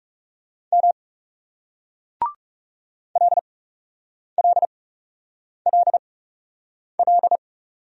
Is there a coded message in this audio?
Yes: Morse code from 0.7 to 7.4 s, heard as a keyed tone, about 41 wpm, spaced as for 8 wpm, 700 Hz. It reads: M LLLL